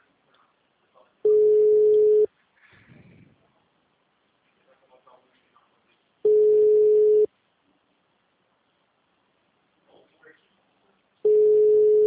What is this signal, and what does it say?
Ringback tone (Europe, 425 Hz), with speech.
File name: out- -2038- - - .33174